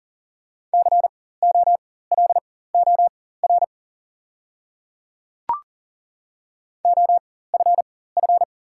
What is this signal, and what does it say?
Morse code, heard as a keyed tone, about 40 wpm, spaced as for 19 wpm, 700 Hz. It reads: COLOR OFF
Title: 4x Character Spacing - Top 300 Words - 40wpm